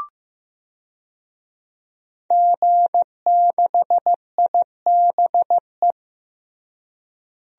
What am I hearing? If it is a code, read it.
Morse code, heard as a keyed tone, about 15 wpm, 700 Hz. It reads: G6IBE
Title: DXCC-100 Call Signs - 5 Characters - Repeated 2x - 15wpm